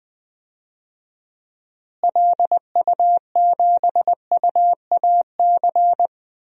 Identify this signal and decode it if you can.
Morse code, heard as a keyed tone, about 21 wpm, 700 Hz. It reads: LU7UAC
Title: DXCC-100 Call Signs - 6 Characters - Repeated 3x - 20wpm